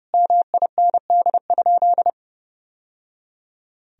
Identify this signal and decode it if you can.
Morse code, heard as a keyed tone, about 30 wpm, 700 Hz. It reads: MIND?
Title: Sentences from Top 1000 Words - Review Part 2 - 30wpm